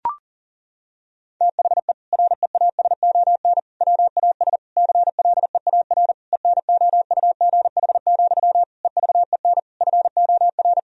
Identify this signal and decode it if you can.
Morse code, heard as a keyed tone, about 40 wpm, 700 Hz. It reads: THE REASON WAS CLEAR ENOUGH, EVEN FOR
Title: Sentences from Top 1000 Words - Review Part 2 - 40wpm